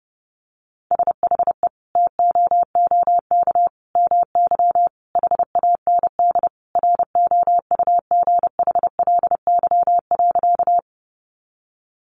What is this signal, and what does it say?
Morse code, heard as a keyed tone, about 30 wpm, 700 Hz. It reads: SHE TOOK MY HAND ROUGHLY.
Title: Sentences from Top 1500 Words 30wpm